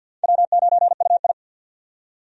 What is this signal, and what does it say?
Morse code, heard as a keyed tone, about 50 wpm, 700 Hz. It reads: W9UI
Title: USA Call Signs - 4 Characters - 50wpm